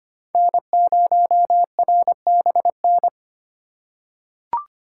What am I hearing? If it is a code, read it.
Morse code, heard as a keyed tone, about 25 wpm, 700 Hz. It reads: N0RBN
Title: USA Call Signs - 5 Characters - 2x Character Spacing - 25wpm